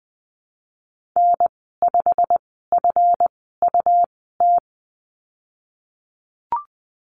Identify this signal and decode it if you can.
Morse code, heard as a keyed tone, about 20 wpm, spaced as for 15 wpm, 700 Hz. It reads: N5FUT
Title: 147 Lesson 24g - Letter y - Call Signs 20wpm